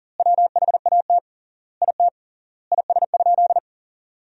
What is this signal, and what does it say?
Morse code, heard as a keyed tone, about 40 wpm, 700 Hz. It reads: WHAT IT IS?
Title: 3x Word-Spacing - Sentences from Top 100 Words - 40wpm